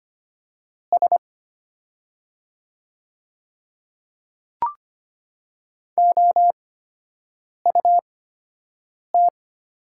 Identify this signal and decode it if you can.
Morse code, heard as a keyed tone, about 25 wpm, spaced as for 7 wpm, 700 Hz. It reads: S OUT
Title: 8x Character Spacing - Top 200 Words - 25wpm